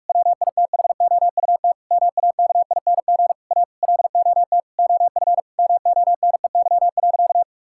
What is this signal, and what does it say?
Morse code, heard as a keyed tone, about 46 wpm, 700 Hz. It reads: WITHOUT MAKING A LOT OF MONEY.
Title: Sentences from Top 1000 Words - Review Part 6 - 45wpm